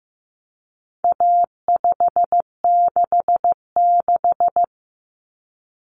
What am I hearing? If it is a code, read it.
Morse code, heard as a keyed tone, about 15 wpm, 700 Hz. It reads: A566
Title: Common Cut Numbers - 4-Digit 15wpm